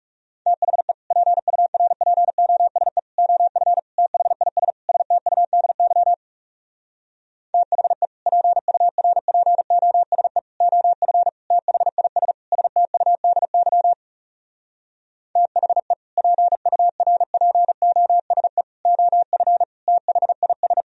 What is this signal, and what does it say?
Morse code, heard as a keyed tone, about 40 wpm, 700 Hz. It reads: THE PURPOSE OF THIS STUDY THE PURPOSE OF THIS STUDY THE PURPOSE OF THIS